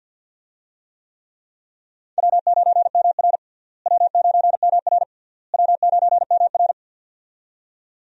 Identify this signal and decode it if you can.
Morse code, heard as a keyed tone, about 49 wpm, 700 Hz. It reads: W9MR W9MR W9MR
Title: USA Call Signs - 4 Characters - Repeated 3x - 50wpm